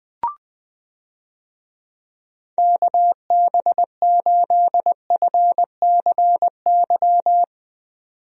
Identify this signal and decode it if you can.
Morse code, heard as a keyed tone, about 20 wpm, 700 Hz. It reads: KB8FCY
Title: USA Call Signs - 6 Characters - 20wpm